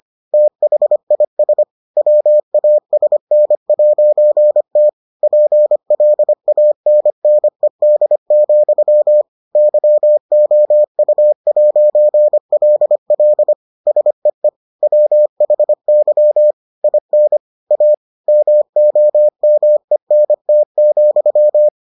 Morse code, heard as a keyed tone, about 25 wpm, 600 Hz. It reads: THIS WASN'T PLANNED, YOU'LL SEE WHY IN A MOMENT,